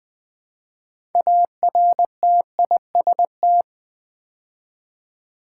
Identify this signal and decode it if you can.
Morse code, heard as a keyed tone, about 20 wpm, 700 Hz. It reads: ARTIST
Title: Build A Word - 2 Letters At A Time - 8-Letter Words - 20wpm